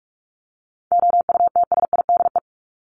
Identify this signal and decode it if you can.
Morse code, heard as a keyed tone, about 44 wpm, 700 Hz. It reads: OUTSIDE
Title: Top 751-1000 Words 45wpm